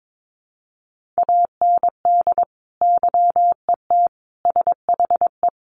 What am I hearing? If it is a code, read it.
Morse code, heard as a keyed tone, about 22 wpm, 700 Hz. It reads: AND YET SHE
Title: Sentences from Top 1500 Words 22wpm